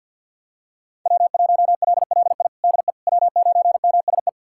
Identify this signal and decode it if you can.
Morse code, heard as a keyed tone, about 46 wpm, 690 Hz. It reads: W1LLI DE W9MSE
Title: USA Callsign DE Callsign - 5 Characters - 50wpm